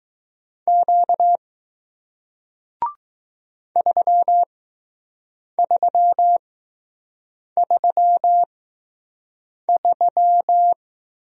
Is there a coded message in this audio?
Yes: Morse code from 0.7 to 10.7 s, heard as a keyed tone, about 20 wpm, spaced as for 6 wpm, 700 Hz. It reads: Q 3333